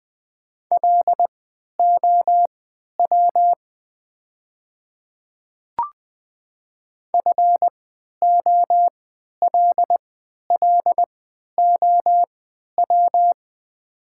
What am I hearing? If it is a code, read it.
Morse code, heard as a keyed tone, about 20 wpm, spaced as for 11 wpm, 700 Hz. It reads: LOW FOLLOW
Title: 3x Character Spacing - Top 500 Words - 20wpm